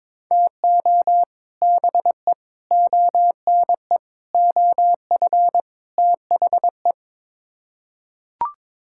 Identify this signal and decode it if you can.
Morse code, heard as a keyed tone, about 21 wpm, 700 Hz. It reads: TO BE ONE OF THE
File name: Build A Phrase - 5-Word Phrases - 22wpm